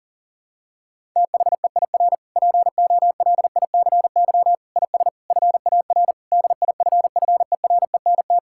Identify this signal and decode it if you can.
Morse code, heard as a keyed tone, about 40 wpm, 700 Hz. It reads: THEIR POLICY IS FAR DIFFERENT
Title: Sentences from Top 1000 Words - Review Part 5 - 40wpm